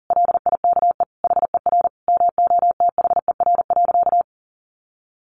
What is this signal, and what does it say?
Morse code, heard as a keyed tone, about 40 wpm, 700 Hz. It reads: LIKE HER MOTHER.
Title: Sentences from Top 1300 Words 40wpm